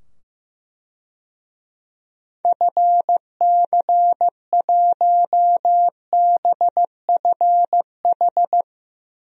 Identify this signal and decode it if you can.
Morse code, heard as a keyed tone, about 15 wpm, 700 Hz. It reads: FC1BFH
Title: DXCC-100 Call Signs - 6 Characters - 15wpm